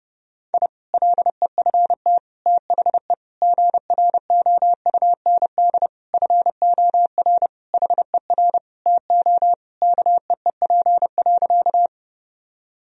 Morse code, heard as a keyed tone, about 30 wpm, 700 Hz. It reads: I LEFT THE GROUND FOR HER TO KEEP.
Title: Sentences from Top 1000 Words - Review Part 2 - 30wpm